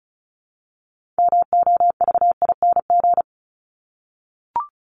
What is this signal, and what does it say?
Morse code, heard as a keyed tone, about 34 wpm, 700 Hz. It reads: MOVING
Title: ING Suffix - Top 100 Words with Base Words 35wpm